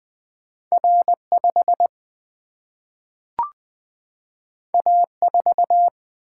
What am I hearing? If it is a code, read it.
Morse code, heard as a keyed tone, about 20 wpm, 700 Hz. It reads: R5 A4
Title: Single-Letter Single-Number 20wpm